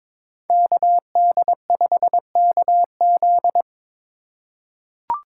Morse code, heard as a keyed tone, about 22 wpm, 700 Hz. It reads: KD5KZ
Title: USA Call Signs - 5 Characters - 4x Character Spacing - 22wpm